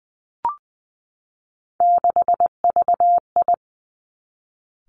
Morse code, heard as a keyed tone, about 20 wpm, 700 Hz. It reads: 6VI
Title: Three Challenging Letter-Number Combinations 20wpm